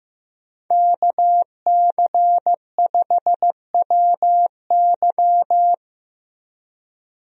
Morse code, heard as a keyed tone, about 15 wpm, 700 Hz. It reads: KC5WY
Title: USA Call Signs - 5 Characters - Repeated 3x - 15wpm